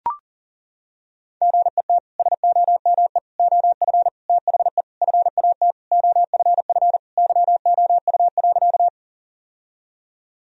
Morse code, heard as a keyed tone, about 40 wpm, 700 Hz. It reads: GET SOME OF THE FAT OFF YOU.
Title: Sentences from Top 1000 Words - Review Part 4 - 40wpm